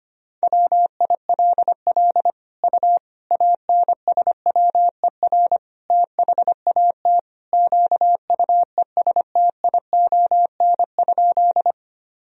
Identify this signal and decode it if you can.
Morse code, heard as a keyed tone, about 25 wpm, 700 Hz. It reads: WILL U ANSWER THAT QUESTION?
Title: Sentences from Top 400 Words and Abbreviations 25wpm